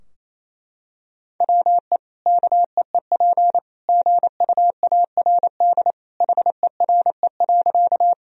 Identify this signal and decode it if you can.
Morse code, heard as a keyed tone, about 28 wpm, 700 Hz. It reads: WE KEEP GUARD HERE.
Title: Sentences from Top 1000 Words - Review Part 4 - 28wpm